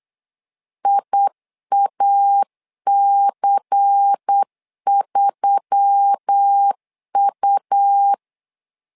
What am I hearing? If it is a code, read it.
Morse code, heard as a keyed tone, about 8 wpm, 800 Hz. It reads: IAC3U
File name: LEZIONI / 5 LBVC / 5 LBVC 8wpm 2935